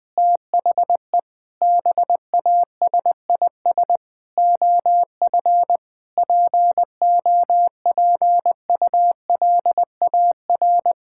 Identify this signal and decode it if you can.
Morse code, heard as a keyed tone, about 20 wpm, 700 Hz. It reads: THE BASIS OF POPULAR
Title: Sentences from Top 1000 Words - Review Part 5 - 20wpm